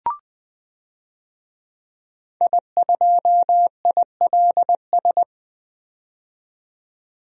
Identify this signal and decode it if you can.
Morse code, heard as a keyed tone, about 20 wpm, 700 Hz. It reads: I2ILS
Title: DXCC-100 Call Signs - 5 Characters - 20wpm